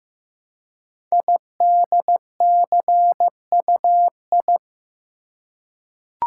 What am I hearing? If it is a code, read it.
Morse code, heard as a keyed tone, about 15 wpm, 700 Hz. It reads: IDCUI
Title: Code Groups - 5 Letter:Numbers - 15wpm